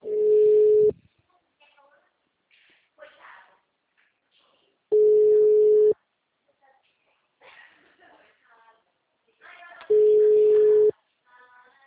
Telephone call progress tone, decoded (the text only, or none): ringback tone